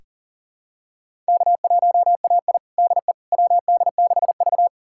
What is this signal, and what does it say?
Morse code, heard as a keyed tone, about 40 wpm, 700 Hz. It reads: K1AI DE WD6V
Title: USA Callsign DE Callsign - 4 Characters - 40wpm